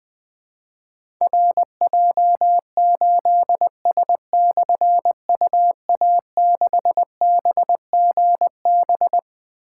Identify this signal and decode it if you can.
Morse code, heard as a keyed tone, about 20 wpm, 700 Hz. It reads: RJ8S/UA6BGB
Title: Difficult Call Signs 20wpm